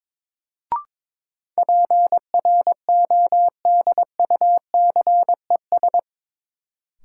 Morse code, heard as a keyed tone, about 22 wpm, 700 Hz. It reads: PRODUCES
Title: Sentences from Top 1000 Words - Review Part 6 - New Word Drill - 22wpm